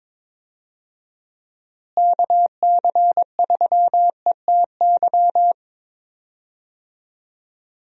Morse code, heard as a keyed tone, about 22 wpm, 700 Hz. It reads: KC3ETY
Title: USA Call Signs - 6 Characters - Repeated 2x - 22wpm